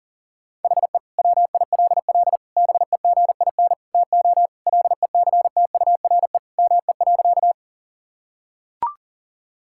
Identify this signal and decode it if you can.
Morse code, heard as a keyed tone, about 37 wpm, 700 Hz. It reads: HE WILL BEGIN TO LECTURE ME.
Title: Sentences from Top 1600 Words 40wpm